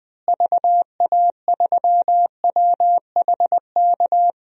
Morse code, heard as a keyed tone, about 21 wpm, 700 Hz. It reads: VA3WHK
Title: DXCC-100 Call Signs - 6 Characters - Repeated 2x - 20wpm